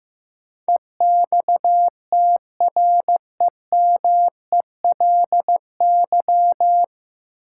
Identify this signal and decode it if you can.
Morse code, heard as a keyed tone, about 15 wpm, 700 Hz. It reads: EXTREMELY
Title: Sentences from Top 1000 Words - Review Part 7 - New Word Drill - 15wpm